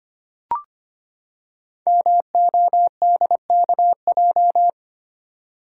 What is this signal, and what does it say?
Morse code, heard as a keyed tone, about 25 wpm, 700 Hz. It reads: MODKJ
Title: Code Groups - 5 Letter:Numbers - 25wpm